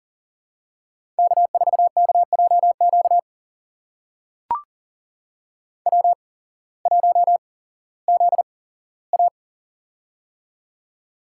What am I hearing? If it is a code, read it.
Morse code, heard as a keyed tone, about 40 wpm, spaced as for 19 wpm, 700 Hz. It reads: K4KJQ W1ZA
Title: USA Call Signs - 8x Character Spacing - 40wpm